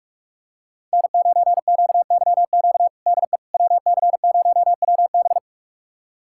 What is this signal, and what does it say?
Morse code, heard as a keyed tone, about 46 wpm, 700 Hz. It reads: N9QYQ DE WC0WB
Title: USA Callsign DE Callsign - 5 Characters - 45wpm